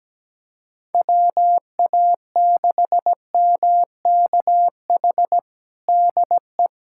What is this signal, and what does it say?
Morse code, heard as a keyed tone, about 17 wpm, 700 Hz. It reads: WA6MKH DE
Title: K1USN SST QSO Practice 17wpm